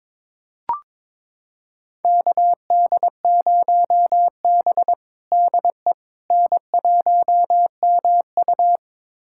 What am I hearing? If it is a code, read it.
Morse code, heard as a keyed tone, about 22 wpm, 700 Hz. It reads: KD0B DE N1MU
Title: USA Callsign DE Callsign - 4 Characters - 22wpm